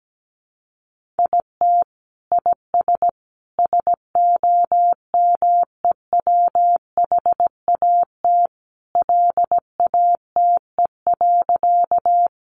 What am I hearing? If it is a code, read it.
Morse code, heard as a keyed tone, about 17 wpm, 700 Hz. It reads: IT IS SOMEWHAT LATE.
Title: Sentences from Top 1300 Words 17wpm